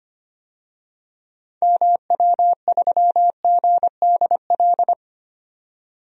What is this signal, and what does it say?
Morse code, heard as a keyed tone, about 25 wpm, 700 Hz. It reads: MW3GDL
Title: DXCC-100 Call Signs - 6 Characters - 25wpm